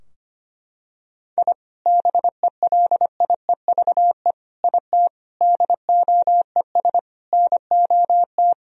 Morse code, heard as a keyed tone, about 25 wpm, 700 Hz. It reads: I BELIEVE IT DOES NOT
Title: Sentences from Top 1000 Words - Review Part 2 - 25wpm